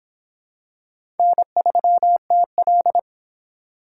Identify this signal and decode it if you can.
Morse code, heard as a keyed tone, about 26 wpm, 700 Hz. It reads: N3TL